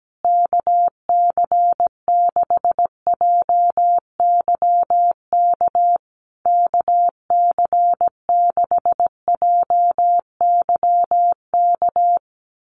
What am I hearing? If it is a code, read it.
Morse code, heard as a keyed tone, about 17 wpm, 700 Hz. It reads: KC6JYK KC6JYK
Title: USA Call Signs Repeated 3x 17wpm